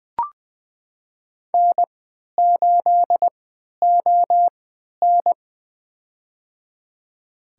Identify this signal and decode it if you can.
Morse code, heard as a keyed tone, about 20 wpm, spaced as for 11 wpm, 700 Hz. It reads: N8ON
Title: USA Call Signs - 4 Characters - 3x Character Spacing - 20wpm